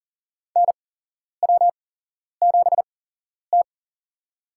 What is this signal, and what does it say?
Morse code, heard as a keyed tone, about 40 wpm, spaced as for 11 wpm, 700 Hz. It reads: NW7T
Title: USA Call Signs - 4 Characters - 8x Character Spacing - 40wpm